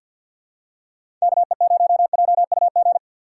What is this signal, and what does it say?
Morse code, heard as a keyed tone, about 51 wpm, 700 Hz. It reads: KE0JUG
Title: USA Call Signs - 6 Characters - 6x Character Spacing - 50wpm